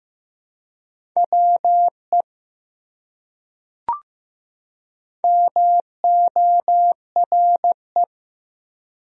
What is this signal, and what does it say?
Morse code, heard as a keyed tone, about 15 wpm, 700 Hz. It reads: WE MORE
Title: Top 100 Words 15wpm